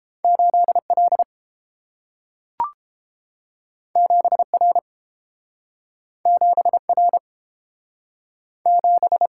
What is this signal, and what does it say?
Morse code, heard as a keyed tone, about 30 wpm, 700 Hz. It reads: 8L 7R 7R 7